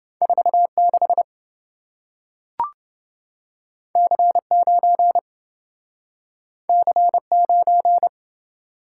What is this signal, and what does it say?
Morse code, heard as a keyed tone, about 27 wpm, 700 Hz. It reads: 46 C9 C9